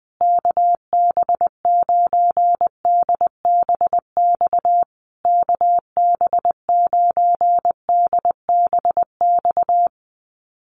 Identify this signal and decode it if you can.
Morse code, heard as a keyed tone, about 20 wpm, 700 Hz. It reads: KB9DBX KB9DBX
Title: USA Call Signs Repeated 3x 20wpm